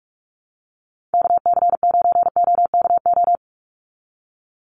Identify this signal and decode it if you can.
Morse code, heard as a keyed tone, about 44 wpm, 700 Hz. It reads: KC9OKO
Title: USA Call Signs 45wpm